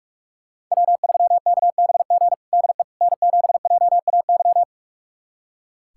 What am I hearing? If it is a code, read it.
Morse code, heard as a keyed tone, about 41 wpm, 700 Hz. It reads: W3KBG DE N7JAY
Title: USA Callsign DE Callsign - 5 Characters - 45wpm